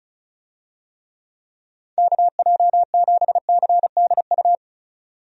Morse code, heard as a keyed tone, about 36 wpm, 700 Hz. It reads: KJ7CDU
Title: USA Call Signs - 6 Characters - 35wpm